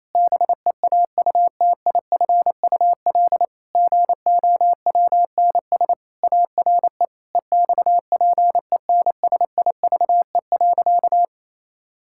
Morse code, heard as a keyed tone, about 28 wpm, 700 Hz. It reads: BEAUTIFUL GOWNS ARE EXPENSIVE.
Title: Pangram Challenge v2 - 28wpm